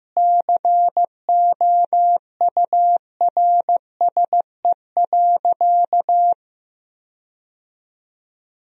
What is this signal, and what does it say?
Morse code, heard as a keyed tone, about 15 wpm, 700 Hz. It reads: COURSE.
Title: 4x Word-Spacing - Sentences from Top 500 Words - 15wpm